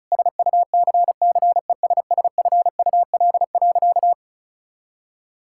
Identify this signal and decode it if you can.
Morse code, heard as a keyed tone, about 34 wpm, 700 Hz. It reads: SUCCESSFUL.
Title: Sentences from Top 1000 Words - Review Part 6 - 35wpm